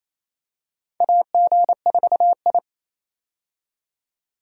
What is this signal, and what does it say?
Morse code, heard as a keyed tone, about 28 wpm, 700 Hz. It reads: AG4I
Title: USA Call Signs - 4 Characters - 28wpm